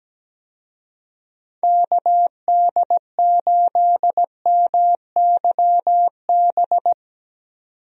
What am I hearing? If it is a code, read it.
Morse code, heard as a keyed tone, about 17 wpm, 700 Hz. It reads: KD8MYB
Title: USA Call Signs - 6 Characters - 17wpm